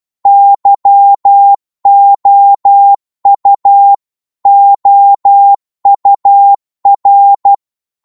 Morse code, heard as a keyed tone, about 12 wpm, 800 Hz. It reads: YOU OUR
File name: Morse / psg-12wpm